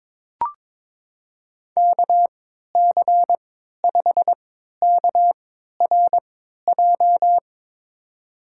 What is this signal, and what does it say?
Morse code, heard as a keyed tone, about 22 wpm, spaced as for 13 wpm, 700 Hz. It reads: KC5KRJ
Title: USA Call Signs - 6 Characters - 3x Character Spacing - 22wpm